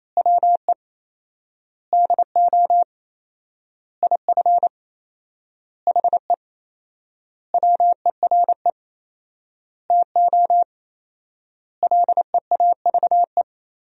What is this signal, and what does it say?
Morse code, heard as a keyed tone, about 28 wpm, 700 Hz. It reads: WE DO IF HE WERE TO LEAVE
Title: 4x Word-Spacing - Sentences from Top 400 Words - 28wpm